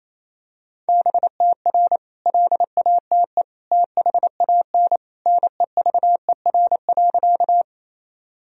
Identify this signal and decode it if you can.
Morse code, heard as a keyed tone, about 27 wpm, 700 Hz. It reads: BTR LATE THAN NEVER.
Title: Sentences from Top 400 Words and Abbreviations 28wpm